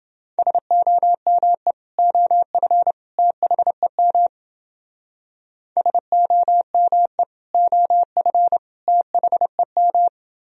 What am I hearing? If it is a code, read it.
Morse code, heard as a keyed tone, about 28 wpm, 700 Hz. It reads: SOME OF THEM SOME OF THEM